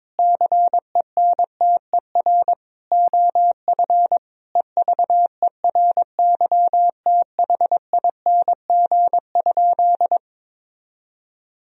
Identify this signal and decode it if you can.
Morse code, heard as a keyed tone, about 21 wpm, 700 Hz. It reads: CENTER OF EVERYTHING?
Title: Sentences from Top 1000 Words - Review Part 2 - 22wpm